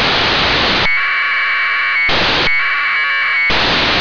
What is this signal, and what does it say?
Packet radio (AX.25, as in APRS).
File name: antelsat 0843utc